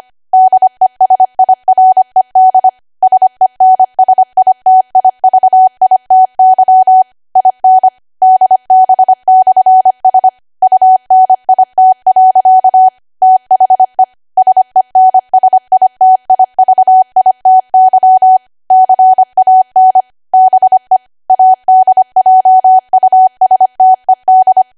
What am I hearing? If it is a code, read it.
Morse code, heard as a keyed tone, about 25 wpm, 750 Hz. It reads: DESIRED SENSITIVITY IN DB/S UNIT. THE SENSITIVITY CAN BE ADJUSTED
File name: Archive / 25 WPM / 25WPM